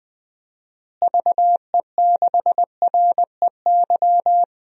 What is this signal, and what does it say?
Morse code, heard as a keyed tone, about 19 wpm, 700 Hz. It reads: VE6REY
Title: DXCC-100 Call Signs - 6 Characters - Repeated 3x - 20wpm